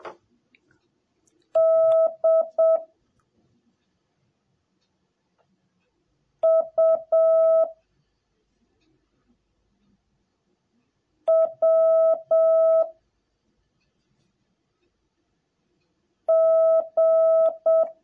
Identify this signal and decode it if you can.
Morse code, heard as a keyed tone, about 7 wpm, 650 Hz. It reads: D U W G